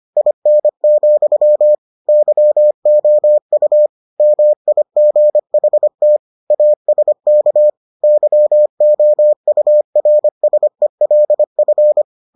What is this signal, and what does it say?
Morse code, heard as a keyed tone, about 25 wpm, 600 Hz. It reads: IN, YOU MIGHT ASK YOURSELF